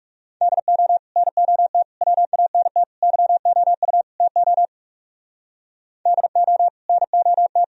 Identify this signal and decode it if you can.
Morse code, heard as a keyed tone, about 37 wpm, 700 Hz. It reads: DO NOT WANT YOU TO DO NOT
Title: Speed-Racing - Sets of 5 Words 30wpm